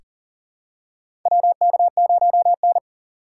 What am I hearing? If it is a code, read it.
Morse code, heard as a keyed tone, about 40 wpm, 700 Hz. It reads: WK0N
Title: USA Call Signs - 4 Characters - 3x Character Spacing - 40wpm